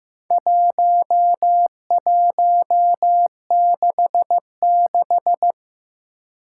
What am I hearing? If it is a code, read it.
Morse code, heard as a keyed tone, about 15 wpm, 700 Hz. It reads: 1166
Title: Four Challenging Letter-Number Combinations 15wpm